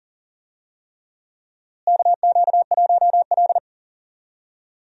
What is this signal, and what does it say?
Morse code, heard as a keyed tone, about 40 wpm, 700 Hz. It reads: KQ1L
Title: USA Call Signs - 4 Characters - Repeated 3x - 40wpm